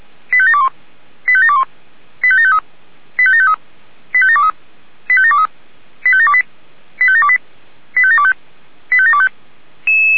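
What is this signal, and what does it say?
Five-tone selective call (ZVEI), heard as quick runs of five tones, tones of about 70 ms. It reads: 85721 85721 85753 85753 85723 85723 85728 85728 85725 85725
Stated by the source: Five-tone ZVEI selective call received by SDR.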